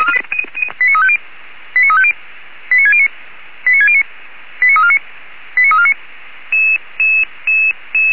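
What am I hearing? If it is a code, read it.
Five-tone selective call (ZVEI), heard as quick runs of five tones, tones of about 70 ms. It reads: 89350 89350 89609 89609 89349 89349
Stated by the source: ZVEI five-tone selective call 89293, received by SDR, logged on the Feuerwehr Orlamünde fire brigade server.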